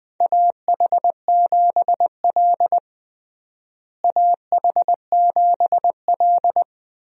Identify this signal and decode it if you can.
Morse code, heard as a keyed tone, about 21 wpm, 700 Hz. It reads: AH7L AH7L
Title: USA Call Signs - 4 Characters - Repeated 3x - 20wpm